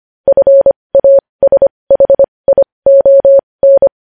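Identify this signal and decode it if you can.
Morse code, heard as a keyed tone, about 25 wpm, spaced as for 20 wpm, 550 Hz. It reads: FASHION
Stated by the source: Morse code news headlines sent at 20.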